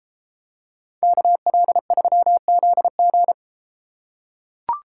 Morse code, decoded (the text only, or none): KL3ZG